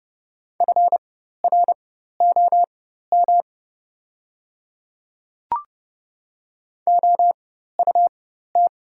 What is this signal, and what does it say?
Morse code, heard as a keyed tone, about 30 wpm, spaced as for 14 wpm, 700 Hz. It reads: FROM OUT
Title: 4x Character Spacing - Top 200 Words - 30wpm